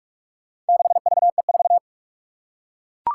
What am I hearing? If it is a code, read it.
Morse code, heard as a keyed tone, about 44 wpm, 700 Hz. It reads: 6VE4
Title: Four Challenging Letter-Number Combinations 45wpm